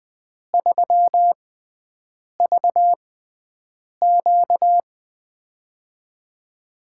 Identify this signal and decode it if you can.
Morse code, heard as a keyed tone, about 20 wpm, spaced as for 7 wpm, 700 Hz. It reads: 3VQ
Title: USA Call Signs - 4 Characters - 6x Character Spacing - 20wpm